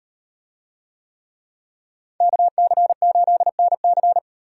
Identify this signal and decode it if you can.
Morse code, heard as a keyed tone, about 38 wpm, 700 Hz. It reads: KC8NC